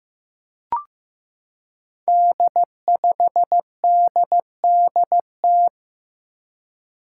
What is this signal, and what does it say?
Morse code, heard as a keyed tone, about 15 wpm, 700 Hz. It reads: D5DDT
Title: Five Challenging Letter-Number Combinations 15wpm